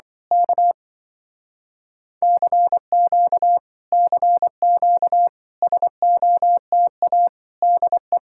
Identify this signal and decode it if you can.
Morse code, heard as a keyed tone, about 24 wpm, 700 Hz. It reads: K CQ CQ SOTA DE
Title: Speed-Racing Common QSO Phrases - 20wpm